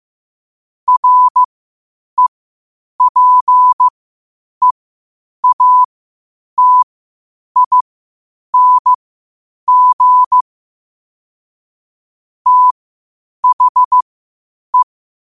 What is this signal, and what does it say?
Morse code, heard as a keyed tone, about 15 wpm, spaced as for 8 wpm, 1000 Hz. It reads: REPEATING THE
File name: qotdfiles / 8wpm